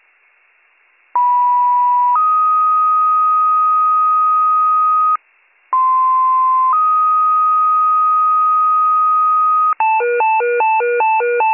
Two-tone page, heard as a one-second tone, then a three-second tone, about 980 then 1250 Hz, about 1040 then 1250 Hz. It is heard from 1.1 to 9.7 s.